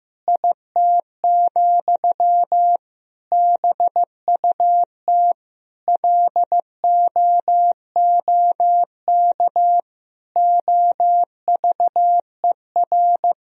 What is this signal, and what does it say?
Morse code, heard as a keyed tone, about 16 wpm, 700 Hz. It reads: IT, BUT LOOK OVER
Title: Sentences from Top 1000 Words - Review Part 2 - 15wpm